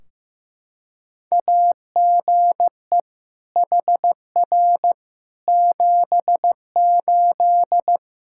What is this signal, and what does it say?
Morse code, heard as a keyed tone, about 15 wpm, 700 Hz. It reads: AGE HR 78
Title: Practice QSO - Exchange 3 - More Info - 15wpm